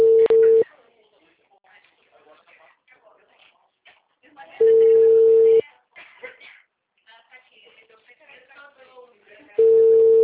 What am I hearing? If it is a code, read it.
Ringback tone (China, 450 Hz), with speech.
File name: out- -2023